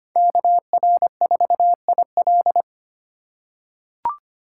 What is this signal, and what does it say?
Morse code, heard as a keyed tone, about 25 wpm, 700 Hz. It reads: KR4IL
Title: USA Call Signs - 5 Characters - 3x Character Spacing - 25wpm